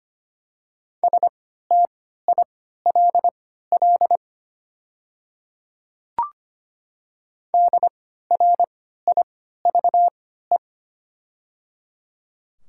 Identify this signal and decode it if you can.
Morse code, heard as a keyed tone, about 25 wpm, spaced as for 14 wpm, 700 Hz. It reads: STILL DRIVE
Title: 3x Character Spacing - Top 500 Words - 25wpm